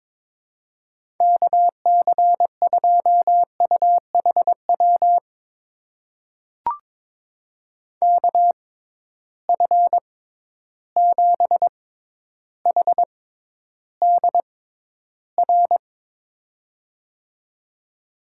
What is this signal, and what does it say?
Morse code, heard as a keyed tone, about 21 wpm, spaced as for 11 wpm, 700 Hz. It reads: KC2UHW KF7HDR
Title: USA Call Signs - 6 Characters - 6x Character Spacing - 22wpm